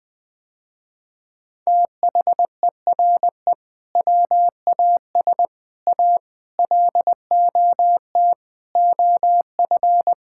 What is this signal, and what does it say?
Morse code, heard as a keyed tone, about 21 wpm, 700 Hz. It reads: THERE WAS A LOT OF
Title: Sentences with 4-Word Phrases - Spoken Beginning - 20wpm